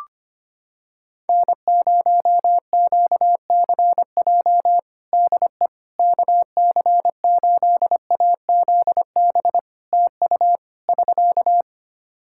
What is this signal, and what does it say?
Morse code, heard as a keyed tone, about 25 wpm, 700 Hz. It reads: N0QCJ DE KC8AZB TU <SK>
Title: Practice QSO - Exchange 4 - Ending - 25wpm